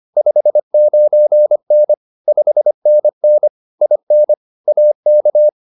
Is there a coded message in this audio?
Yes: Morse code from 0.2 to 5.5 s, heard as a keyed tone, about 25 wpm, 600 Hz. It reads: 59N 5NN IN AK